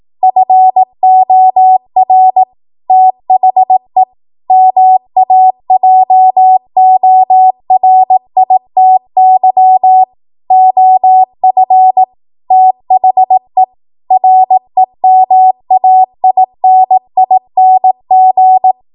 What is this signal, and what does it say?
Morse code, heard as a keyed tone, about 18 wpm, 750 Hz. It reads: FOR THE MAJORITY OF THE REMAINING
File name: Morse / 18WPM